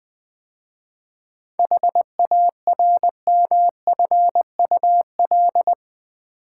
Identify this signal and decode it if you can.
Morse code, heard as a keyed tone, about 20 wpm, 700 Hz. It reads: HARMFUL
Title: 7 Letter Words - First 3 Letters Given - 20wpm